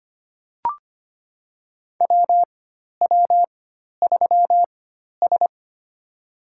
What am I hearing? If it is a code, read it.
Morse code, heard as a keyed tone, about 25 wpm, spaced as for 12 wpm, 700 Hz. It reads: WW3S
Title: USA Call Signs - 4 Characters - 4x Character Spacing - 25wpm